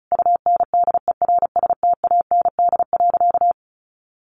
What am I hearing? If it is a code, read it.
Morse code, heard as a keyed tone, about 34 wpm, 700 Hz. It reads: UNDERSTAND.
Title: Sentences from Top 600 Words 35wpm